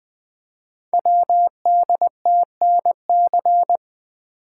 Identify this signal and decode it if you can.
Morse code, heard as a keyed tone, about 20 wpm, 700 Hz. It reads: WDTNC
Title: Code Groups - 5 Letters - 20wpm